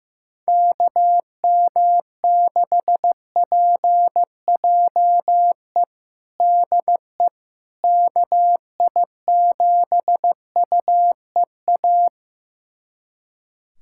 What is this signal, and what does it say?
Morse code, heard as a keyed tone, about 15 wpm, 700 Hz. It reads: KM6PJE DE KI7UEA
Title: USA Callsign DE Callsign 15wpm